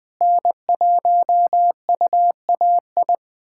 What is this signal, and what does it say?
Morse code, heard as a keyed tone, about 20 wpm, 700 Hz. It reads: N1UAI
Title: USA Callsign DE Callsign - 5 Characters - 20wpm